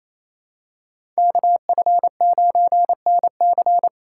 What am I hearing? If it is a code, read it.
Morse code, heard as a keyed tone, about 28 wpm, 700 Hz. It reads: KF9NC
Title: USA Call Signs - 5 Characters - 8x Character Spacing - 28wpm